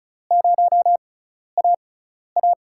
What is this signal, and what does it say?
Morse code, heard as a keyed tone, about 34 wpm, spaced as for 12 wpm, 700 Hz. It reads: 0AA
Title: USA Call Signs - 5 Characters - 6x Character Spacing - 35wpm